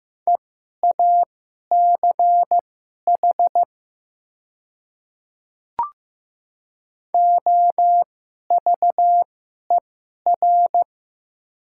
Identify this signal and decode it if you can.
Morse code, heard as a keyed tone, about 15 wpm, spaced as for 11 wpm, 700 Hz. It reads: EACH OVER
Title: 2x Character Spacing - Top 100 Words - 15wpm